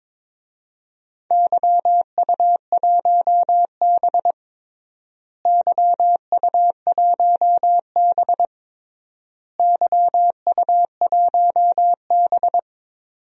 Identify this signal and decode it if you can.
Morse code, heard as a keyed tone, about 22 wpm, 700 Hz. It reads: YU1B YU1B YU1B
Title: DXCC-100 Call Signs - 4 Characters - Repeated 3x - 22wpm